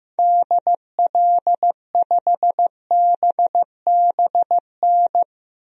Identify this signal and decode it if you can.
Morse code, heard as a keyed tone, about 15 wpm, 700 Hz. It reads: DL5BBN
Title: DXCC-100 Call Signs - 6 Characters - 15wpm